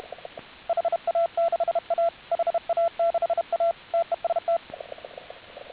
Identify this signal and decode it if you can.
Morse code, heard as a keyed tone, about 32 wpm, 690 Hz. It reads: HA6A HA6A TEST